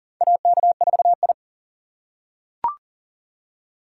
Morse code, heard as a keyed tone, about 40 wpm, 700 Hz. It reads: AK4I
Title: USA Call Signs - 4 Characters - 6x Character Spacing - 40wpm